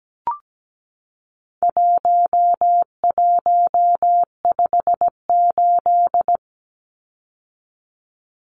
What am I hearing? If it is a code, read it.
Morse code, heard as a keyed tone, about 17 wpm, 700 Hz. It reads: 1158
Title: Numbers - 4-Digit 17wpm